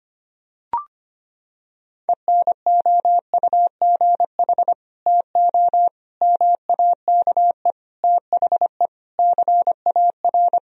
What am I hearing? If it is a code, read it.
Morse code, heard as a keyed tone, about 25 wpm, 700 Hz. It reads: ENOUGH TO MAKE THE CAR
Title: Sentences from Top 1000 Words - Review Part 1 - 25wpm